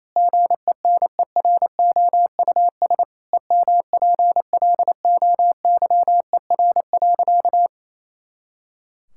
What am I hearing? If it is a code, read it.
Morse code, heard as a keyed tone, about 28 wpm, 700 Hz. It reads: GENEROUS EMPLOYER.
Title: Pangram Challenge v2 - 28wpm